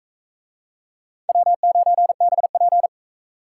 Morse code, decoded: W9BP

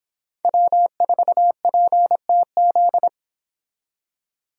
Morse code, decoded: W4PTZ

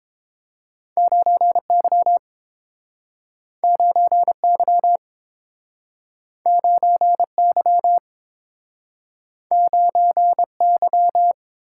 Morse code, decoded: 9Y 9Y 9Y 9Y

